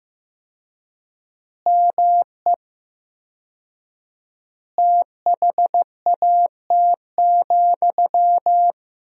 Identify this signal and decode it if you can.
Morse code, heard as a keyed tone, about 15 wpm, 700 Hz. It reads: ME THAT,
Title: 4x Word-Spacing - Sentences from Top 500 Words - 15wpm